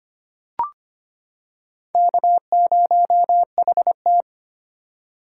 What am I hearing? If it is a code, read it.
Morse code, heard as a keyed tone, about 25 wpm, 700 Hz. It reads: K0HT
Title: K1USN SST QSO Practice - 25wpm